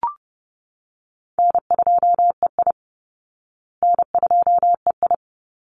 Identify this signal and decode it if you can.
Morse code, heard as a keyed tone, about 30 wpm, 700 Hz. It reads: N2EI N2EI
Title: Easy Call Signs Repeated 2x - 30wpm